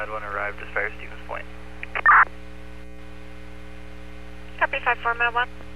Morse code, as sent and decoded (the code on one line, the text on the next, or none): none